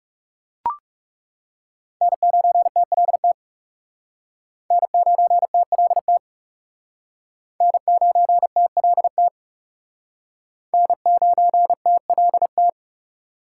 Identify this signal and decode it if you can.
Morse code, heard as a keyed tone, about 35 wpm, 700 Hz. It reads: N9TLT N9TLT N9TLT N9TLT
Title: Speed Racing - USA 1x3 Call Signs 30wpm